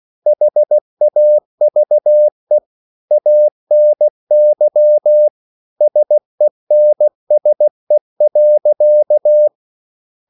Morse code, heard as a keyed tone, about 15 wpm, 600 Hz. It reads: HAVE ANY SENSE.